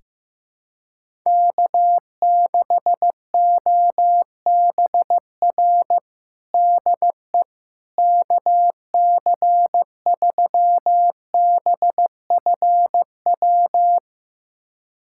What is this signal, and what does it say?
Morse code, heard as a keyed tone, about 16 wpm, 700 Hz. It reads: K6OBR DE KC3BFW
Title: USA Callsign DE Callsign 15wpm